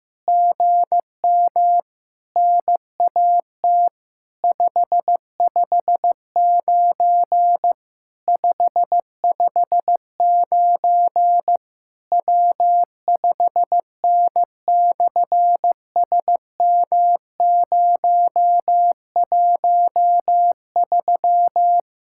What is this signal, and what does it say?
Morse code, heard as a keyed tone, about 15 wpm, 700 Hz. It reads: GM NAT 559 559 W5N/SM013
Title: SOTA QSO Practice 15wpm